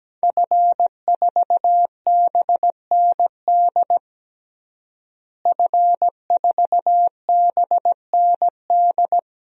Morse code, heard as a keyed tone, about 17 wpm, 700 Hz. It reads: F4BND F4BND
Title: DXCC-100 Call Signs - 5 Characters - Repeated 3x - 17wpm